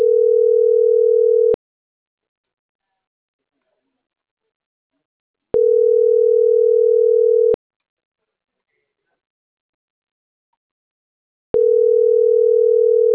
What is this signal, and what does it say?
Ringback tone (North America), 440 + 480 Hz.